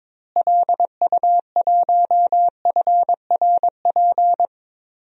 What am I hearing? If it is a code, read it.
Morse code, heard as a keyed tone, about 22 wpm, 700 Hz. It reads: LU1FRP
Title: DXCC-100 Call Signs - 6 Characters - Repeated 2x - 22wpm